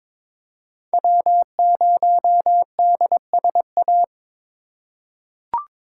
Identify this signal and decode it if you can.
Morse code, heard as a keyed tone, about 21 wpm, 700 Hz. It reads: W0DSA
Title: USA Call Signs - 4x Character Spacing - 22wpm